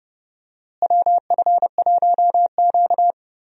Morse code, heard as a keyed tone, about 31 wpm, 700 Hz. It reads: WF1Q
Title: USA Call Signs - 4 Characters - Repeated 3x - 30wpm